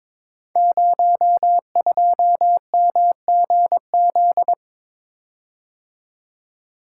Morse code, heard as a keyed tone, about 22 wpm, 700 Hz. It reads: 02MGZ
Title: Code Groups - 5 Letter:Numbers - 22wpm